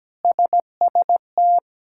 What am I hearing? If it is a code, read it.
Morse code, heard as a keyed tone, about 17 wpm, 700 Hz. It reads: SST
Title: K1USN SST QSO Practice - 17wpm